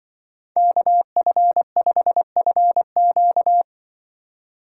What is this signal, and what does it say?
Morse code, heard as a keyed tone, about 23 wpm, 700 Hz. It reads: KF5FQ
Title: Speed Racing - USA 2x2 Call Signs 20wpm